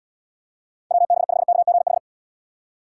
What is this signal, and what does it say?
Morse code, heard as a keyed tone, about 76 wpm, 700 Hz. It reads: WB5KGH